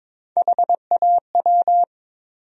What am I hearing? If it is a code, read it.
Morse code, heard as a keyed tone, about 22 wpm, 700 Hz. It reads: HAW
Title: USA Call Sign Postfixes - 3 Letters 22wpm